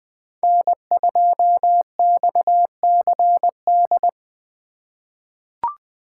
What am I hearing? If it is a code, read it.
Morse code, heard as a keyed tone, about 20 wpm, 700 Hz. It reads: N2XCD
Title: USA Call Signs - 2x Character Spacing - 20wpm